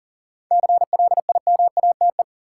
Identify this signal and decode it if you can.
Morse code, heard as a keyed tone, about 40 wpm, 700 Hz. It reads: CLIMATE
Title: Sentences from Top 1600 Words - New Word Drill - 40wpm